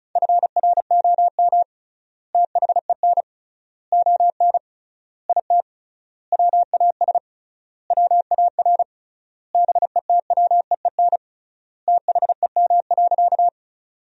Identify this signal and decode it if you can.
Morse code, heard as a keyed tone, about 34 wpm, 700 Hz. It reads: FROM THEN ON IT WAS WAR BETWEEN THEM.
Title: 3x Word-Spacing - Sentences from Top 400 Words - 35wpm